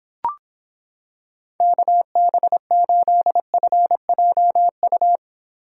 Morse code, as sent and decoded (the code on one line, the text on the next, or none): -.- -... ---.. ..-. .--- ..-
KB8FJU